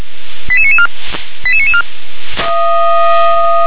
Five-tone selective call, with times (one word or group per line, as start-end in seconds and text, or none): 0.5-0.9 88994
1.5-1.8 88994